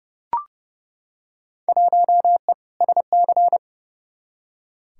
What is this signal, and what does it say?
Morse code, heard as a keyed tone, about 31 wpm, 700 Hz. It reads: 1E SC
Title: Field Day Exchanges - 30wpm